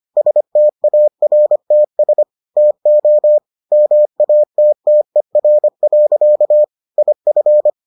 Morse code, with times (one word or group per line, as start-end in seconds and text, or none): 0.2-2.2 STARTS
2.6-3.4 TO
3.7-6.6 MATTER.
7.0-7.7 IF